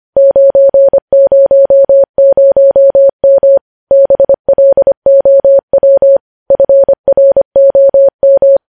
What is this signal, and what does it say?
Morse code, heard as a keyed tone, about 25 wpm, 550 Hz. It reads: 900M BLOW FROM
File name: podcast / News headlines-1x-25x25